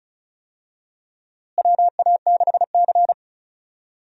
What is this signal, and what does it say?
Morse code, heard as a keyed tone, about 36 wpm, 700 Hz. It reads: WA6C
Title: USA Call Signs - 4 Characters - Repeated 3x - 35wpm